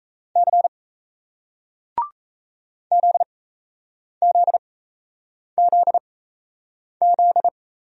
Morse code, heard as a keyed tone, about 38 wpm, spaced as for 8 wpm, 700 Hz. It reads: C ZZZZ